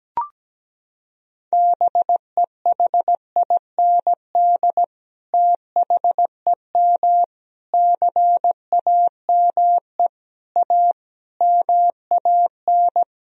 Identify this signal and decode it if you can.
Morse code, heard as a keyed tone, about 17 wpm, 700 Hz. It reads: BEHIND THEM CAME A MAN
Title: Sentences from Top 1000 Words - Review Part 4 - 17wpm